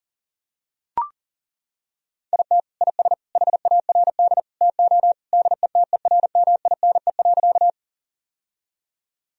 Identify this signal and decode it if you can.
Morse code, heard as a keyed tone, about 40 wpm, 700 Hz. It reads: IT IS HARD TO DETERMINE.
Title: Sentences from Top 1000 Words - Review Part 4 - 40wpm